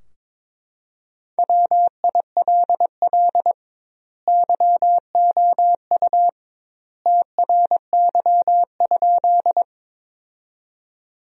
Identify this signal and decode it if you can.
Morse code, heard as a keyed tone, about 22 wpm, 700 Hz. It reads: WILL YOU TRY?
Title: 2x Word-Spacing - Sentences from Top 200 Words - 22wpm